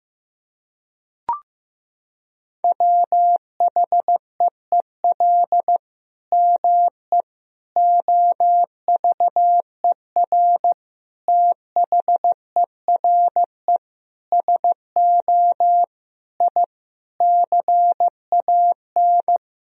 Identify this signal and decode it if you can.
Morse code, heard as a keyed tone, about 15 wpm, 700 Hz. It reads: WHEEL ME OVER THERE SO I CAN
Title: Sentences from Top 1000 Words - Review Part 3 - 15wpm